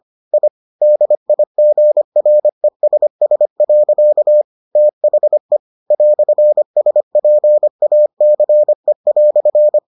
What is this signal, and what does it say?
Morse code, heard as a keyed tone, about 25 wpm, 600 Hz. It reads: I DIGRESS. THE "SPACE"